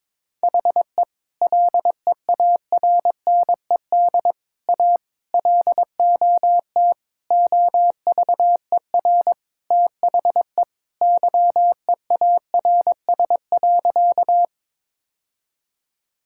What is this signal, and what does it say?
Morse code, heard as a keyed tone, about 22 wpm, 700 Hz. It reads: HE LEARNED A LOT OVER THE YEARS.
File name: practice / 22wpm / Sentences with 3-Word Phrases - 22wpm